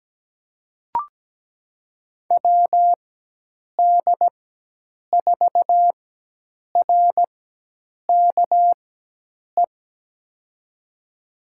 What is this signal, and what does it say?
Morse code, heard as a keyed tone, about 17 wpm, spaced as for 8 wpm, 700 Hz. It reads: WD4RKE
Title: USA Call Signs - 6 Characters - 4x Character Spacing - 17wpm